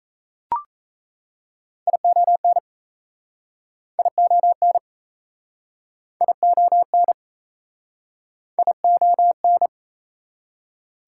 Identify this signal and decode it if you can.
Morse code, heard as a keyed tone, about 40 wpm, 700 Hz. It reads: ION ION ION ION